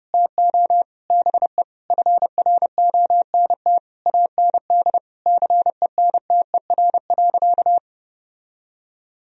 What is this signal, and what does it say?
Morse code, heard as a keyed tone, about 32 wpm, 700 Hz. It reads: TO BE FRONT AND CENTER.
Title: Sentences from Top 1000 Words - Review Part 2 - 30wpm